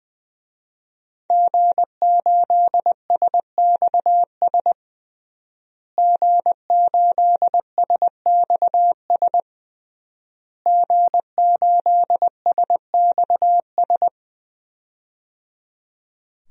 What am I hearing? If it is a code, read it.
Morse code, heard as a keyed tone, about 20 wpm, 700 Hz. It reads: G8SXS G8SXS G8SXS
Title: DXCC-100 Call Signs - 5 Characters - Repeated 3x - 20wpm